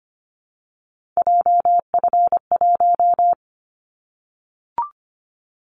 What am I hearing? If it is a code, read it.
Morse code, heard as a keyed tone, about 25 wpm, 700 Hz. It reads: JF1
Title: Callsign Prefixes for Top 25 DXCC - 25wpm